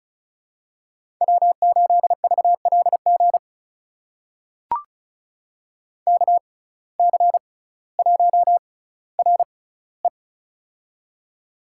Morse code, heard as a keyed tone, about 36 wpm, spaced as for 18 wpm, 700 Hz. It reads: W8VLG KC1RE
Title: USA Call Signs - 5 Characters - 6x Character Spacing - 35wpm